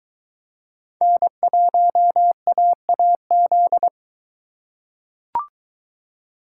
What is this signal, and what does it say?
Morse code, heard as a keyed tone, about 23 wpm, 700 Hz. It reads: N1AAZ